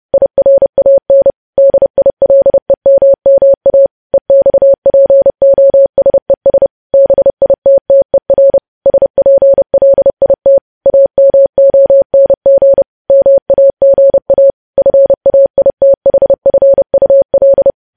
Morse code, heard as a keyed tone, about 30 wpm, 550 Hz. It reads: IRAN DILEMMA EXPOSES BITTER SPLIT AMONG MAGA FAITHFUL
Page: news headlines in morse code at 30 wpm